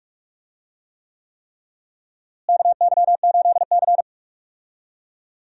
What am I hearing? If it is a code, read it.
Morse code, heard as a keyed tone, about 46 wpm, 700 Hz. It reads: KY8C
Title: USA Call Signs - 4 Characters - Repeated 3x - 45wpm